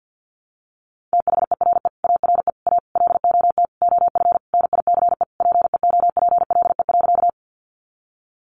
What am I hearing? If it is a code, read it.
Morse code, heard as a keyed tone, about 48 wpm, 700 Hz. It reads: THERE ARE A LOT OF NICE PEOPLE.
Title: Sentences from Top 1100 Words 50wpm